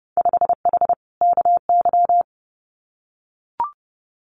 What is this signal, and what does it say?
Morse code, heard as a keyed tone, about 30 wpm, 700 Hz. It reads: 5H KY
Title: Winter Field Day Exchanges Encore 30wpm